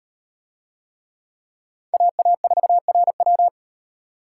Morse code, heard as a keyed tone, about 36 wpm, 700 Hz. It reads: AA4RW